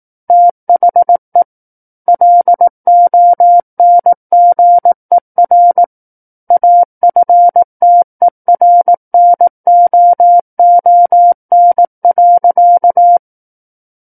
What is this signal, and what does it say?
Morse code, heard as a keyed tone, about 18 wpm, 700 Hz. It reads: THE LONGER AFTERNOON.
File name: 18 WPM